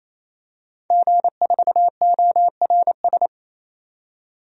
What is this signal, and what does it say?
Morse code, heard as a keyed tone, about 27 wpm, 690 Hz. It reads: G4ORS
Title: DXCC-100 Call Signs - 5 Characters - 28wpm